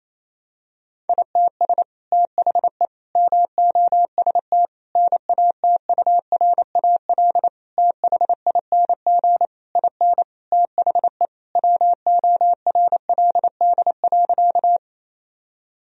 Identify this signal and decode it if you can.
Morse code, heard as a keyed tone, about 28 wpm, 700 Hz. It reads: ITS THE MOST NATURAL THING IN THE WORLD.
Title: Sentences from Top 1000 Words - Review Part 4 - 28wpm